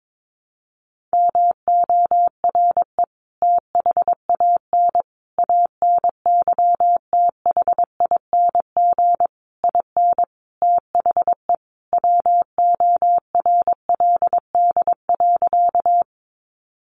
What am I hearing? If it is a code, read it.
Morse code, heard as a keyed tone, about 22 wpm, 700 Hz. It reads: MORE THAN ANYTHING IN THE WORLD.
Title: Sentences from Top 1000 Words 22wpm